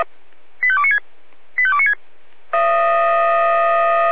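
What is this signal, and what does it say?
Five-tone selective call (ZVEI), heard as quick runs of five tones, tones of about 70 ms. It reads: 85296 85296
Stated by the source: SDR-received ZVEI five-tone selective call.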